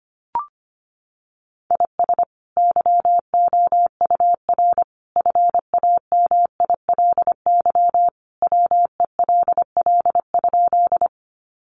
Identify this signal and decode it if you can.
Morse code, heard as a keyed tone, about 25 wpm, 700 Hz. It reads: IS YOUR FAMILY WELL?
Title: Sentences from Top 400 Words 25wpm